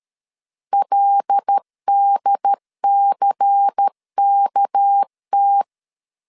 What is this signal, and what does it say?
Morse code, heard as a keyed tone, about 13 wpm, 800 Hz. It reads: LDCKT
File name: LEZIONI / 6 PKX0 / 6 PK 12wpm 2006